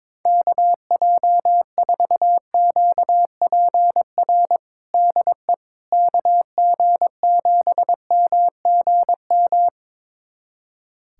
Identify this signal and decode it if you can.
Morse code, heard as a keyed tone, about 22 wpm, 700 Hz. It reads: KJ4QPR DE KG7MGM
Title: USA Callsign DE Callsign 22wpm